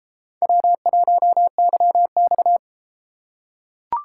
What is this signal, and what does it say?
Morse code, heard as a keyed tone, about 33 wpm, 700 Hz. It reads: W1YX